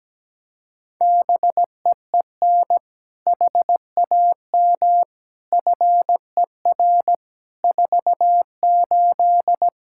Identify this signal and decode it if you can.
Morse code, heard as a keyed tone, about 17 wpm, 700 Hz. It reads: BEEN HAM FER 48
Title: Practice QSO - Exchange 3 - More Info - 17wpm